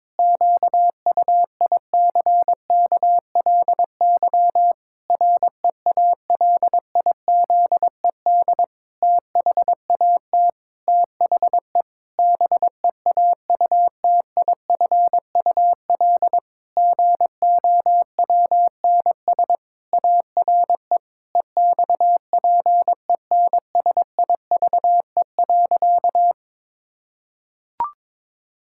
Morse code, heard as a keyed tone, about 22 wpm, 700 Hz. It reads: QUICKLY REALIZED THAT THE BEAUTIFUL GOWNS ARE EXPENSIVE.
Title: Pangram Challenge v2 - 22wpm